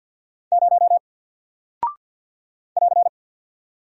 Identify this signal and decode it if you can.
Morse code, heard as a keyed tone, about 48 wpm, 700 Hz. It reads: 0 <AR>
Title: Single Character (Letter:Number:Punctuation:Prosign) Alt 50wpm